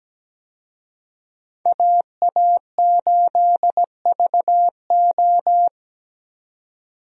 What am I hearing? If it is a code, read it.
Morse code, heard as a keyed tone, about 17 wpm, 700 Hz. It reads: AA8VO
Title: USA Call Signs - 5 Characters - 17wpm